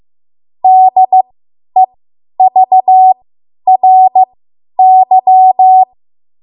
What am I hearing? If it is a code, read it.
Morse code, heard as a keyed tone, about 15 wpm, spaced as for 10 wpm, 750 Hz. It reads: DEVRY
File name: Morse / 10WPM